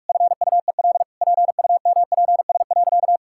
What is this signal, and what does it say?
Morse code, heard as a keyed tone, about 46 wpm, 700 Hz. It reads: FUEL PUMPS.
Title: Pangram Challenge v2 - 45wpm